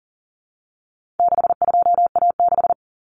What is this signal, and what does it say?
Morse code, heard as a keyed tone, about 40 wpm, 700 Hz. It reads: 62A6
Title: Common Cut Numbers - 4-Digit 40wpm